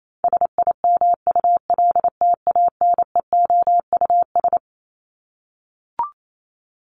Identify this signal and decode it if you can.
Morse code, heard as a keyed tone, about 28 wpm, 700 Hz. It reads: SIMULTANEOUS
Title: LY Suffix - Top 100 Words with Base Words 28wpm